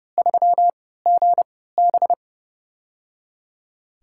Morse code, heard as a keyed tone, about 30 wpm, 700 Hz. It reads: 3 G B
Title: USA Call Signs - 4 Characters - 3x Character Spacing - 30wpm